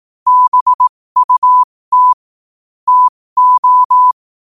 Morse code, heard as a keyed tone, about 18 wpm, spaced as for 15 wpm, 1000 Hz. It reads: BUT TO